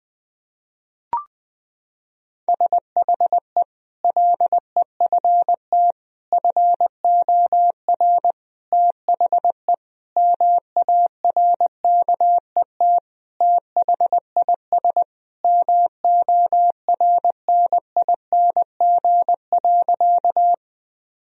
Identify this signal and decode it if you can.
Morse code, heard as a keyed tone, about 19 wpm, 700 Hz. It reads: SHE LEFT FOR THE MARKET THIS MORNING.
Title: Sentences from Top 1000 Words - Review Part 3 - 20wpm